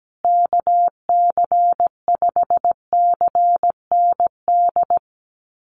Morse code, heard as a keyed tone, about 17 wpm, 700 Hz. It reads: KC5CND
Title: USA Call Signs Repeated 2x 17wpm